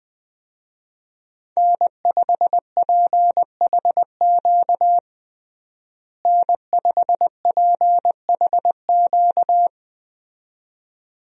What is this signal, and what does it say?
Morse code, heard as a keyed tone, about 20 wpm, 700 Hz. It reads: N5PHQ N5PHQ
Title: USA Call Signs - 5 Characters - Repeated 2x - 20wpm